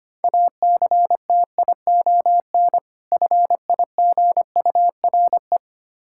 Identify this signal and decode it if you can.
Morse code, heard as a keyed tone, about 25 wpm, 700 Hz. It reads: ACTION FIGURE
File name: practice / 25wpm / Difficult Words with Hints - 25wpm